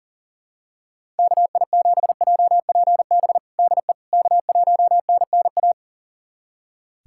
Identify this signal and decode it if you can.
Morse code, heard as a keyed tone, about 37 wpm, 700 Hz. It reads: KI7JPB DE K1NNA
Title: USA Callsign DE Callsign 40wpm